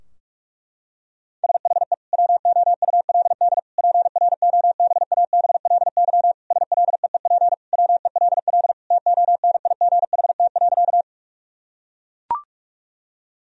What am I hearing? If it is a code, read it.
Morse code, heard as a keyed tone, about 41 wpm, 700 Hz. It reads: SHE WOULD PROBABLY SLEEP WELL TONIGHT.
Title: Sentences from Top 1000 Words - Review Part 4 - 45wpm